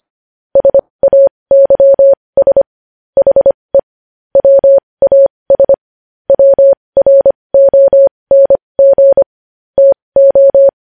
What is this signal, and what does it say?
Morse code, heard as a keyed tone, about 25 wpm, spaced as for 20 wpm, 550 Hz. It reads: SAYS HE WAS WRONG TO
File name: podcast / News headlines-1x-25x20